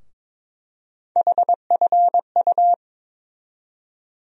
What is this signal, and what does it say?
Morse code, heard as a keyed tone, about 21 wpm, 700 Hz. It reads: HFU
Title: USA Call Sign Postfixes - 3 Letters 22wpm